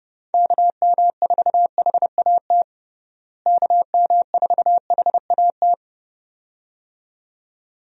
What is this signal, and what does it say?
Morse code, heard as a keyed tone, about 30 wpm, 700 Hz. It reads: KM4HAT KM4HAT
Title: USA Call Signs - 6 Characters - Repeated 2x - 30wpm